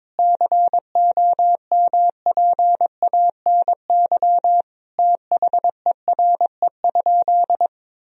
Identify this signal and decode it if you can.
Morse code, heard as a keyed tone, about 23 wpm, 700 Hz. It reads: COMPANY THERE?
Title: Sentences from Top 1000 Words - Review Part 3 - 22wpm